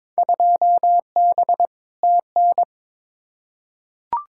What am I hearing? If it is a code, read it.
Morse code, heard as a keyed tone, about 22 wpm, 700 Hz. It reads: 2B TN
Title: Field Day Exchanges Encore - 22wpm